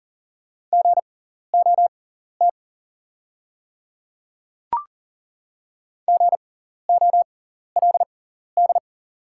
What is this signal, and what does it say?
Morse code, heard as a keyed tone, about 39 wpm, spaced as for 14 wpm, 690 Hz. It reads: GOT GOLD
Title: 6x Character Spacing - Top 500 Words - 40wpm